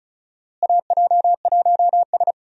Morse code, heard as a keyed tone, about 34 wpm, 700 Hz. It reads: AJ1S
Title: USA Call Signs - 4 Characters - Repeated 2x - 35wpm